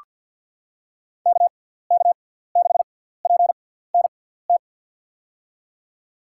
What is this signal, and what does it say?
Morse code, heard as a keyed tone, about 48 wpm, spaced as for 17 wpm, 700 Hz. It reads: KK6PNT
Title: USA Call Signs - 6 Characters - 6x Character Spacing - 50wpm